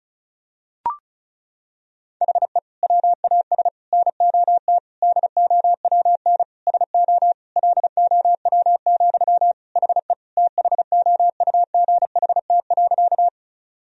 Morse code, heard as a keyed tone, about 34 wpm, 700 Hz. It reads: HE WAS NOT DOWN SO LOW, HE THOUGHT.
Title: Sentences from Top 1000 Words - Review Part 1 - 35wpm